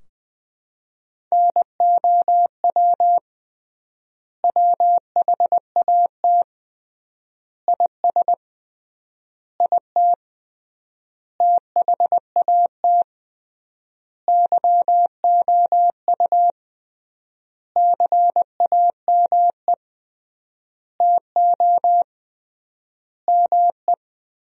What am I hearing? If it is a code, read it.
Morse code, heard as a keyed tone, about 21 wpm, 700 Hz. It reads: NOW WHAT IS IT THAT YOU CAME TO ME
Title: 3x Word-Spacing - Sentences from Top 200 Words - 20wpm